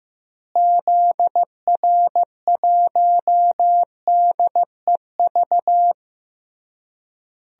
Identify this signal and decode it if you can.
Morse code, heard as a keyed tone, about 16 wpm, 700 Hz. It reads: ZR1DEV
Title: DXCC-100 Call Signs - 6 Characters - Repeated 3x - 15wpm